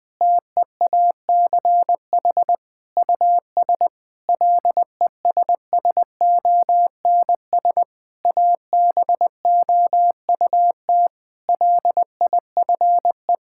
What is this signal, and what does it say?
Morse code, heard as a keyed tone, about 20 wpm, 700 Hz. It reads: TEACH US LESSONS ABOUT LIFE
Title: Difficult Sentences with Hints - 20wpm